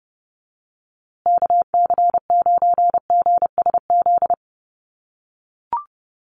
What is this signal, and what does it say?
Morse code, heard as a keyed tone, about 30 wpm, 700 Hz. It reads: KC9GSZ
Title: USA Call Signs Repeated 3x 30wpm